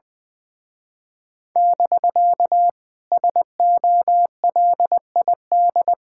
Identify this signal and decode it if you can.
Morse code, heard as a keyed tone, about 21 wpm, 700 Hz. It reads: <BK> SOLID
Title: Speed-Racing Common QSO Phrases Encore - 20wpm